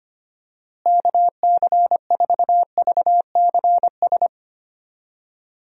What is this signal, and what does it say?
Morse code, heard as a keyed tone, about 25 wpm, 700 Hz. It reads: KC4VCS
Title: USA Call Signs - 6 Characters - Repeated 2x - 25wpm